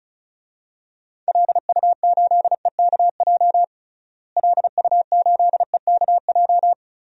Morse code, heard as a keyed tone, about 36 wpm, 700 Hz. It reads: LU8EKJ LU8EKJ
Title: DXCC-100 Call Signs - 6 Characters - Repeated 3x - 35wpm